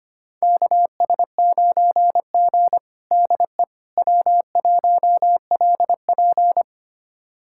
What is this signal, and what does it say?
Morse code, heard as a keyed tone, about 25 wpm, 700 Hz. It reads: KS9G DE W1LP
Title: USA Callsign DE Callsign - 4 Characters - 25wpm